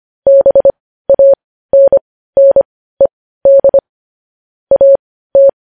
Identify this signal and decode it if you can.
Morse code, heard as a keyed tone, about 25 wpm, spaced as for 15 wpm, 550 Hz. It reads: BANNED AT
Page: News Headlines in Morse Code at 15 WPM